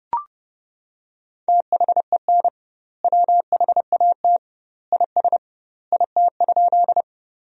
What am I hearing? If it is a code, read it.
Morse code, heard as a keyed tone, about 30 wpm, 700 Hz. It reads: THEN WHAT IS IT?
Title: 2x Word-Spacing - Sentences from Top 100 Words - 30wpm